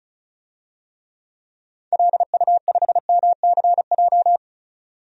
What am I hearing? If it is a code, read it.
Morse code, heard as a keyed tone, about 36 wpm, 700 Hz. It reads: LU5MCJ
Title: DXCC-100 Call Signs - 6 Characters - 35wpm